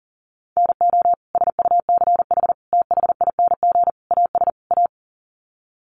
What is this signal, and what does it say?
Morse code, heard as a keyed tone, about 40 wpm, 700 Hz. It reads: NO SUCH THING AS A
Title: Sets of 5 Words Encore 40wpm